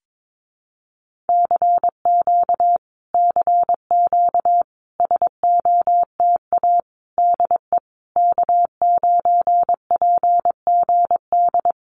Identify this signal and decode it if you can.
Morse code, heard as a keyed tone, about 22 wpm, 700 Hz. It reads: CQ CQ SOTA DE K9PGD
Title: Common QSO Phrases Encore 22wpm